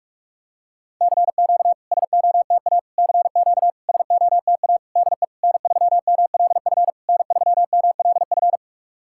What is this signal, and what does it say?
Morse code, heard as a keyed tone, about 44 wpm, 690 Hz. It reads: CQ SOTA CQ SOTA DE N3MLF N3MLF
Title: SOTA QSO Practice 45wpm